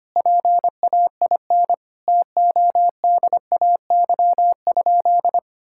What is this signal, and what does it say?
Morse code, heard as a keyed tone, about 25 wpm, 700 Hz. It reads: PAIN TODAY?
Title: Sentences from Top 1600 Words 25wpm